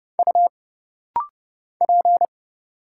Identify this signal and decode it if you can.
Morse code, heard as a keyed tone, about 30 wpm, 700 Hz. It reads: U P
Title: Single Letter-Number Alt - Rapid-Fire 30wpm